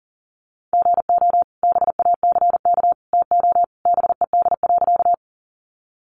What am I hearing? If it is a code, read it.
Morse code, heard as a keyed tone, about 40 wpm, 700 Hz. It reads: GO BACK TO BED.
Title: Sentences from Top 500 Words 40wpm